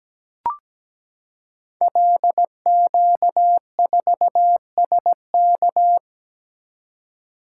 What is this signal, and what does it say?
Morse code, heard as a keyed tone, about 17 wpm, 700 Hz. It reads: LQ4SK
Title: Code Groups - 5 Letter:Numbers - 17wpm